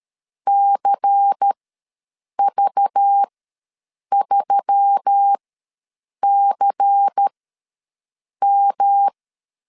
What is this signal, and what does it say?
Morse code, heard as a keyed tone, about 13 wpm, spaced as for 7 wpm, 800 Hz. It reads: CV3CM
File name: LEZIONI / 5 LBVC / 5 LBVC ws 12wpm 3159